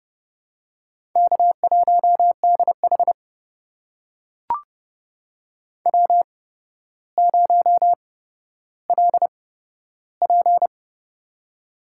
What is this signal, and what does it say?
Morse code, heard as a keyed tone, about 32 wpm, spaced as for 13 wpm, 700 Hz. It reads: K1DH W0LP
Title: USA Call Signs - 4 Characters - 8x Character Spacing - 30wpm